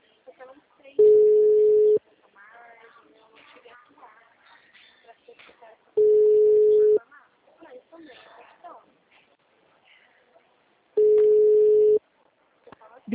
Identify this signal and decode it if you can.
Ringback tone (Europe, 425 Hz), with speech.
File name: out- -2012- - - .40464